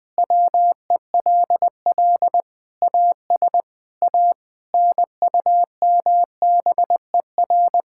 Morse code, heard as a keyed tone, about 21 wpm, 700 Hz. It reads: WELL AS A NUMBER
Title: Sentences with 6-Word Phrases - 20wpm